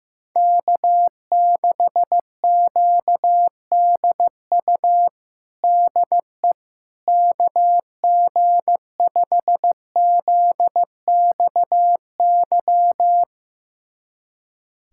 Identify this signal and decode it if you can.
Morse code, heard as a keyed tone, about 16 wpm, 700 Hz. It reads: K6QDU DE KG5ZXY
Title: USA Callsign DE Callsign 15wpm